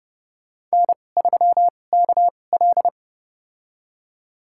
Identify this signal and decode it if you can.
Morse code, heard as a keyed tone, about 30 wpm, spaced as for 22 wpm, 700 Hz. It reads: N3KL
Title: USA Call Signs - 4 Characters - 2x Character Spacing - 30wpm